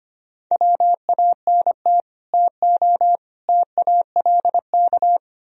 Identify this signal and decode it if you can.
Morse code, heard as a keyed tone, about 33 wpm, 700 Hz. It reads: WANT TO TALK